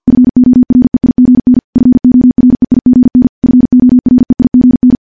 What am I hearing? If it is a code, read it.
Morse code, heard as a keyed tone, about 50 wpm, 250 Hz. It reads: FOREIGN FOREIGN FOREIGN
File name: morse / morse-training-50WPM